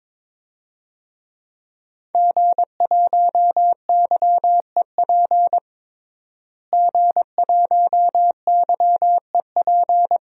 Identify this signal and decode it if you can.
Morse code, heard as a keyed tone, about 21 wpm, 700 Hz. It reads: G1YEP G1YEP
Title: DXCC-100 Call Signs - 5 Characters - Repeated 2x - 22wpm